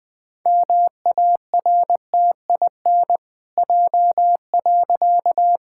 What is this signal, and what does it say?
Morse code, heard as a keyed tone, about 20 wpm, 700 Hz. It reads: MARTIN J.
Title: Pangram Challenge v2 - 20wpm